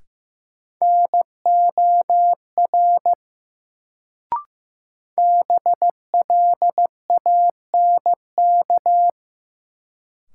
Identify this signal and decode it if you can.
Morse code, heard as a keyed tone, about 15 wpm, 700 Hz. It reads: NOR BLANK
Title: Sentences from Top 1500 Words - New Word Drill Alt - 15wpm